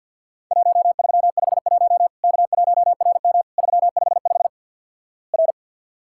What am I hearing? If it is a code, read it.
Morse code, heard as a keyed tone, about 49 wpm, 700 Hz. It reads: 1351 K1RM 355 R
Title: Worked All Europe DX Contest - QTC Exchange with Repeats - 50wpm